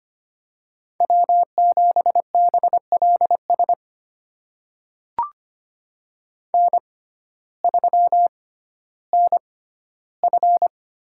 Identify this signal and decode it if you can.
Morse code, heard as a keyed tone, about 25 wpm, spaced as for 14 wpm, 700 Hz. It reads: W7BLS N3NF
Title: USA Call Signs - 5 Characters - 6x Character Spacing - 25wpm